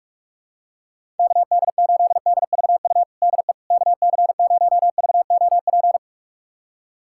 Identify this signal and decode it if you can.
Morse code, heard as a keyed tone, about 41 wpm, 700 Hz. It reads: KD8DVU DE KC0VOP
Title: USA Callsign DE Callsign 45wpm